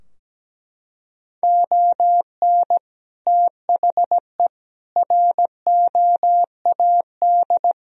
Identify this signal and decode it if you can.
Morse code, heard as a keyed tone, about 17 wpm, 700 Hz. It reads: ON THE ROAD
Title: Sentences from Top 1000 Words - Review Part 2 - 17wpm